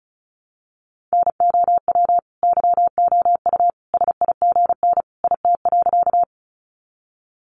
Morse code, heard as a keyed tone, about 34 wpm, 700 Hz. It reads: NOW YOU SIGN IT.
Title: Sentences from Top 1000 Words 35wpm